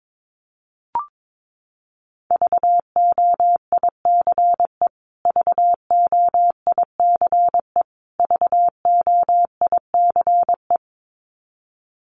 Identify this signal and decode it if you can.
Morse code, heard as a keyed tone, about 22 wpm, 700 Hz. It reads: VOICE VOICE VOICE
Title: Top 500 Words Repeated 22wpm